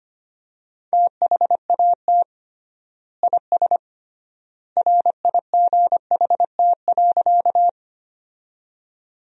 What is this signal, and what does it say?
Morse code, heard as a keyed tone, about 25 wpm, 700 Hz. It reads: THAT IS RIGHT.
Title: 3x Word-Spacing - Sentences from Top 200 Words - 25wpm